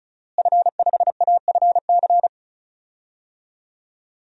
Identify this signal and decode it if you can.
Morse code, heard as a keyed tone, about 36 wpm, 700 Hz. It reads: F5AFC
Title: DXCC-100 Call Signs - 5 Characters - Repeated 3x - 35wpm